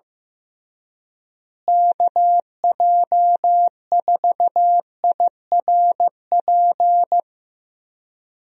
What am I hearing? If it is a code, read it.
Morse code, heard as a keyed tone, about 15 wpm, 700 Hz. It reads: KJ4IRP
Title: USA Call Signs - 6 Characters - Repeated 3x - 15wpm